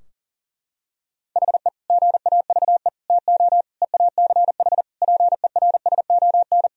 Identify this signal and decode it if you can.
Morse code, heard as a keyed tone, about 40 wpm, 700 Hz. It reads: HE GAVE TO EACH PERSON
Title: Sentences from Top 1000 Words - Review Part 4 - 40wpm